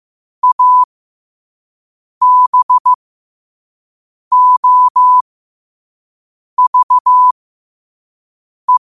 Morse code, heard as a keyed tone, about 15 wpm, spaced as for 5 wpm, 1000 Hz. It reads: ABOVE
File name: qotdfiles / 5wpm